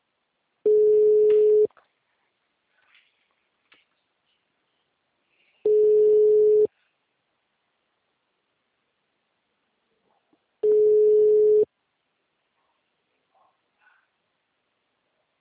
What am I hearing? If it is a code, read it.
Ringback tone (Europe, 425 Hz).